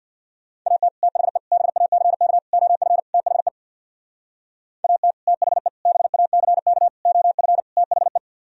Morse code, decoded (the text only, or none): AT THE BACK OF THE AT THE BACK OF THE